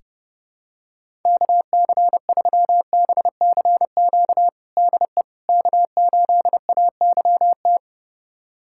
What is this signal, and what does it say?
Morse code, heard as a keyed tone, about 32 wpm, 700 Hz. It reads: KC3BCQ DE K8AYT
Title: USA Callsign DE Callsign 30wpm